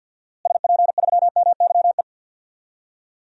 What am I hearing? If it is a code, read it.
Morse code, heard as a keyed tone, about 50 wpm, 700 Hz. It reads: SP3MYE
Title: DXCC-100 Call Signs - 6 Characters - Repeated 3x - 50wpm